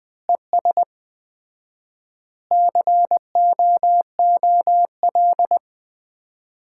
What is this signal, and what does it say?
Morse code, heard as a keyed tone, about 20 wpm, 700 Hz. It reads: ES COOL
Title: Common QSO Phrases - 4x Word-Spacing - 20wpm